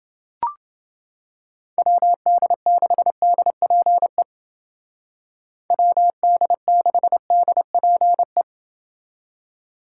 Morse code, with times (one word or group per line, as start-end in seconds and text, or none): 1.8-4.2 WD6DPE
5.7-8.4 WD6DPE